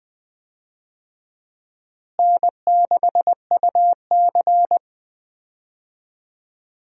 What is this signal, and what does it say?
Morse code, heard as a keyed tone, about 20 wpm, 700 Hz. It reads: N6UC
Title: USA Call Signs - 4 Characters - Repeated 3x - 20wpm